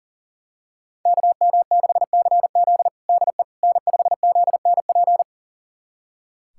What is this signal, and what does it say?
Morse code, heard as a keyed tone, about 40 wpm, 700 Hz. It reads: KM6CZ DE N5ZNP
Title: USA Callsign DE Callsign - 5 Characters - 40wpm